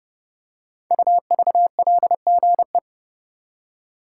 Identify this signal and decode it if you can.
Morse code, heard as a keyed tone, about 29 wpm, 700 Hz. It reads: UVLGE
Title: Code Groups - 5 Letter:Numbers - 30wpm